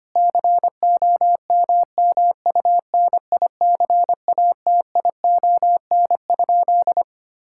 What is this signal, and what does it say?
Morse code, heard as a keyed tone, about 25 wpm, 700 Hz. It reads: COMMUNICATION?
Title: Sentences from Top 1000 Words - Review Part 6 - 25wpm